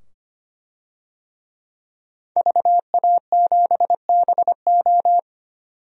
Morse code, heard as a keyed tone, about 25 wpm, 700 Hz. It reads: VA7BO
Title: DXCC-100 Call Signs - 5 Characters - 25wpm